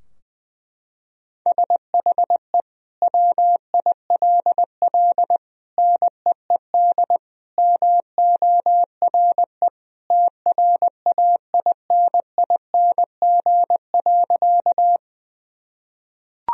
Morse code, heard as a keyed tone, about 20 wpm, 700 Hz. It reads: SHE WILL NEED MORE TRAINING.
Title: Sentences from Top 1000 Words - Review Part 4 - 20wpm